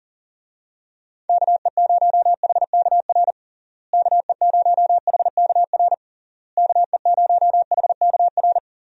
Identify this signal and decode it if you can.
Morse code, heard as a keyed tone, about 40 wpm, 700 Hz. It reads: KE0HKR KE0HKR KE0HKR
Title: USA Call Signs - 6 Characters - Repeated 3x - 40wpm